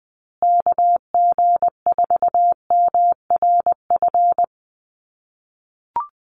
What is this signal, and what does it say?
Morse code, heard as a keyed tone, about 20 wpm, 700 Hz. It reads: KG4MRF
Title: USA Call Signs Repeated 2x 20wpm